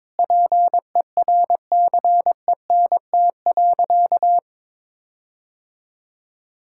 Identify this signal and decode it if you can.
Morse code, heard as a keyed tone, about 22 wpm, 700 Hz. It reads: PERCENT.
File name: practice / 22wpm / Sentences with 8-Word Phrases - 22wpm